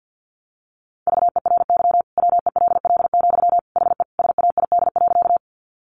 Morse code, heard as a keyed tone, about 50 wpm, 700 Hz. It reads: VERY WELL, HE SAID.
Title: Sentences from Top 200 Words 50wpm